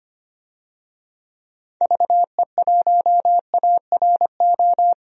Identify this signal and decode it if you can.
Morse code, heard as a keyed tone, about 25 wpm, 700 Hz. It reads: VE1ARO
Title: DXCC-100 Call Signs - 6 Characters - 25wpm